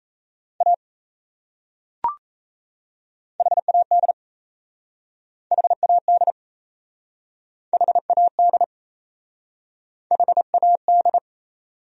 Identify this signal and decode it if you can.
Morse code, heard as a keyed tone, about 37 wpm, 700 Hz. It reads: A HAD HAD HAD HAD